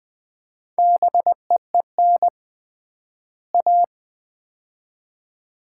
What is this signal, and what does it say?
Morse code, heard as a keyed tone, about 20 wpm, 700 Hz. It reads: BEEN A
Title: 3x Word-Spacing - Sets of 2 Words - 20wpm